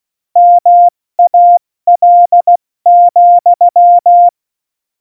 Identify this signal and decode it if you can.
Morse code, heard as a keyed tone, about 16 wpm, 700 Hz. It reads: MAL,
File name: Morsekurs / Witze / Witz 18 August 2020, 70 F80